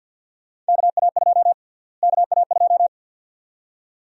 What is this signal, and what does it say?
Morse code, heard as a keyed tone, about 49 wpm, 700 Hz. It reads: KA2 KA2
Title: USA Call Sign Prefixes - Repeated 2x - 50wpm